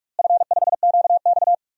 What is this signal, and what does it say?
Morse code, heard as a keyed tone, about 41 wpm, 700 Hz. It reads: F5QX
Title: DXCC-100 Call Signs - 4 Characters - Repeated 2x - 45wpm